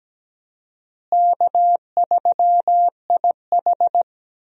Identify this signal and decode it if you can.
Morse code, heard as a keyed tone, about 17 wpm, 700 Hz. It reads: K3IH
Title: USA Call Signs - 4 Characters - Repeated 3x - 17wpm